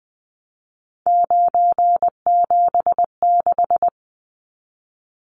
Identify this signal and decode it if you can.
Morse code, heard as a keyed tone, about 19 wpm, 700 Hz. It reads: 976
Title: Numbers - 3-Digit 20wpm